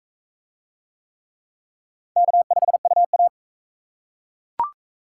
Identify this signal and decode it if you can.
Morse code, heard as a keyed tone, about 37 wpm, 700 Hz. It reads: K5UA